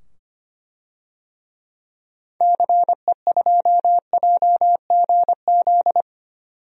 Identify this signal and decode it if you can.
Morse code, heard as a keyed tone, about 25 wpm, 700 Hz. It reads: CE2JGZ
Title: DXCC-100 Call Signs - 6 Characters - 25wpm